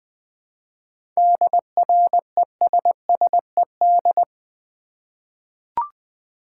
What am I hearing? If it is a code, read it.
Morse code, heard as a keyed tone, about 21 wpm, 700 Hz. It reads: DRESSED
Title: 7 Letter Words - First 3 Letters Given - 20wpm